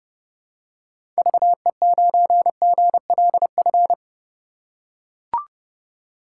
Morse code, heard as a keyed tone, about 30 wpm, 700 Hz. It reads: VE9GLF
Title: K1USN SST QSO Practice - 30wpm